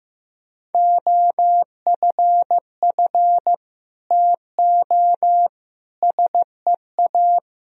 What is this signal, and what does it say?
Morse code, heard as a keyed tone, about 15 wpm, 700 Hz. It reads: OFF TO SEA
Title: Sentences from Top 1000 Words - Review Part 1 - 15wpm